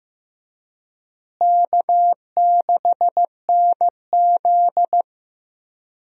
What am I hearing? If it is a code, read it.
Morse code, heard as a keyed tone, about 15 wpm, 700 Hz. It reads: K6NZ
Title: USA Call Signs - 4 Characters - Repeated 3x - 15wpm